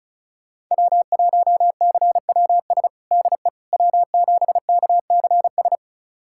Morse code, heard as a keyed tone, about 34 wpm, 700 Hz. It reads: W1CWS DE W7KCS
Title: USA Callsign DE Callsign - 5 Characters - 35wpm